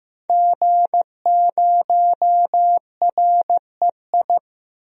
Morse code, heard as a keyed tone, about 15 wpm, 700 Hz. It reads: G0REI
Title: DXCC-100 Call Signs - 5 Characters - 15wpm